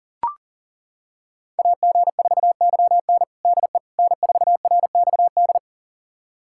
Morse code, heard as a keyed tone, about 40 wpm, 700 Hz. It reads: AG4YN DE N4RXD
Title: USA Callsign DE Callsign - 5 Characters - 40wpm